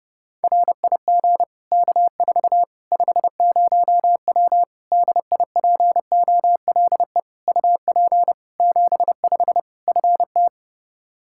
Morse code, heard as a keyed tone, about 30 wpm, 700 Hz. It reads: RIG K4 50W DIPOLE UP 75 FT
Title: Practice QSO - Exchange 2 - Info - 30wpm